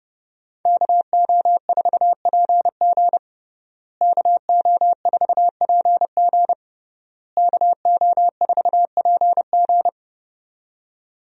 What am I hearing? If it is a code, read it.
Morse code, heard as a keyed tone, about 30 wpm, 700 Hz. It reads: KO4PG KO4PG KO4PG
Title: USA Call Signs - 5 Characters - Repeated 3x - 30wpm